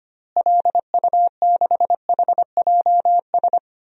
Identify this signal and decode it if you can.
Morse code, heard as a keyed tone, about 25 wpm, 700 Hz. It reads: LU6HJS
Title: DXCC-100 Call Signs - 6 Characters - Repeated 3x - 25wpm